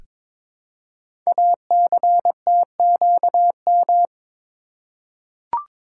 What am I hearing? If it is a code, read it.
Morse code, heard as a keyed tone, about 22 wpm, 700 Hz. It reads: ACTQM
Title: Code Groups - 5 Letters - 22wpm